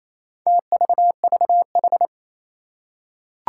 Morse code, heard as a keyed tone, about 27 wpm, 700 Hz. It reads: TVVH
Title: Four Challenging Letter-Number Combinations 28wpm